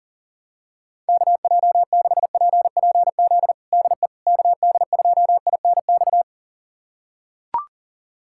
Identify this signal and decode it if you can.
Morse code, heard as a keyed tone, about 37 wpm, 700 Hz. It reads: KJ6PPZ DE KD2INX
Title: USA Callsign DE Callsign 40wpm